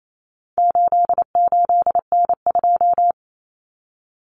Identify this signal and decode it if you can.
Morse code, heard as a keyed tone, about 28 wpm, 700 Hz. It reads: 88N2
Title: Common Cut Numbers - 4-Digit 28wpm